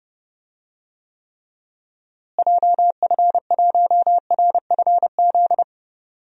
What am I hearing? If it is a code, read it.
Morse code, heard as a keyed tone, about 30 wpm, 700 Hz. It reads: JF1RFZ
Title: DXCC-100 Call Signs - 6 Characters - Repeated 2x - 30wpm